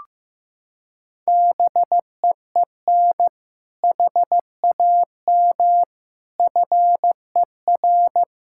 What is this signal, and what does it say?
Morse code, heard as a keyed tone, about 15 wpm, 700 Hz. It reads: BEEN HAM FER
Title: Practice QSO - Exchange 3 - More Info - 15wpm